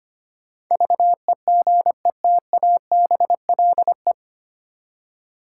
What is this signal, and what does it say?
Morse code, heard as a keyed tone, about 25 wpm, 700 Hz. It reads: VEGETABLE
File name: Nine Letter Words - 25wpm